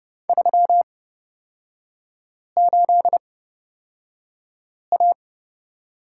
Morse code, heard as a keyed tone, about 30 wpm, spaced as for 5 wpm, 700 Hz. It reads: 38A
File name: Lesson 37e - Letter x - Three Characters 30wpm